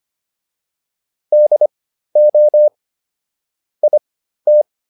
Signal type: Morse code

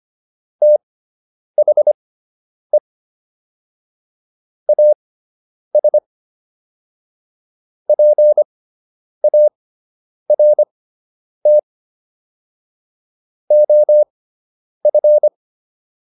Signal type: Morse code

